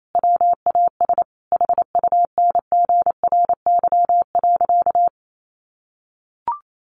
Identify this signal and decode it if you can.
Morse code, heard as a keyed tone, about 28 wpm, 700 Hz. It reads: WAS HUNGRY.
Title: Sentences from Top 1400 Words 28wpm